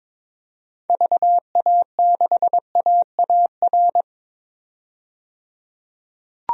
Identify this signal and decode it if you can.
Morse code, heard as a keyed tone, about 22 wpm, 700 Hz. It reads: VA6AAR
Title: DXCC-100 Call Signs - 6 Characters - 22wpm